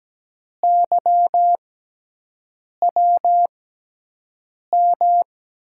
Morse code, heard as a keyed tone, about 17 wpm, spaced as for 6 wpm, 700 Hz. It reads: YWM
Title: USA Call Signs - 6x Character Spacing - 17wpm